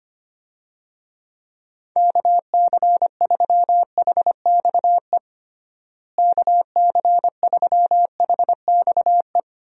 Morse code, heard as a keyed tone, about 25 wpm, 700 Hz. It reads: KC3HXE KC3HXE
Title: USA Call Signs - 6 Characters - Repeated 3x - 25wpm